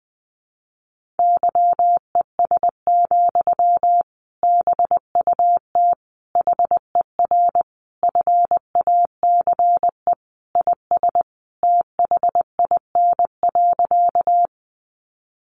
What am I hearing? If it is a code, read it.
Morse code, heard as a keyed tone, about 20 wpm, 700 Hz. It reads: YES, BUT HER FACE IS THIN.
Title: Sentences from Top 1200 Words 20wpm